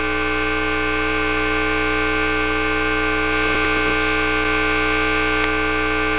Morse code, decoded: none